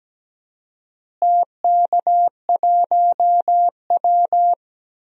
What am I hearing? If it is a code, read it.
Morse code, heard as a keyed tone, about 17 wpm, 700 Hz. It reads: TK1W
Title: DXCC-100 Call Signs - 4 Characters - Repeated 2x - 17wpm